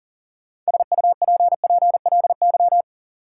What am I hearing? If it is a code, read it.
Morse code, heard as a keyed tone, about 38 wpm, 700 Hz. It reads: SUPPLY
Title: Sentences from Top 1000 Words - Review Part 4 - New Word Drill - 40wpm